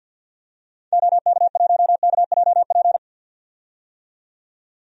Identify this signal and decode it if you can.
Morse code, heard as a keyed tone, about 51 wpm, 700 Hz. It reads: OK1KJP
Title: DXCC-100 Call Signs - 6 Characters - 50wpm